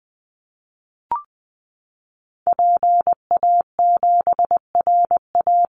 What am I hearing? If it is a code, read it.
Morse code, heard as a keyed tone, about 19 wpm, 700 Hz. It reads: PA7RA
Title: Call Signs Repeated 3x - 20wpm